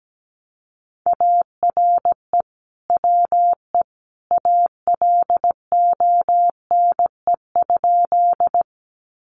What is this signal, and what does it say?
Morse code, heard as a keyed tone, about 17 wpm, 700 Hz. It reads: ARE WE ALONE?
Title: Sentences from Top 1000 Words 17wpm